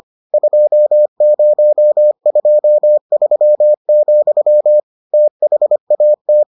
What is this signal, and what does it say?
Morse code, heard as a keyed tone, about 25 wpm, 600 Hz. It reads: 2023, THAT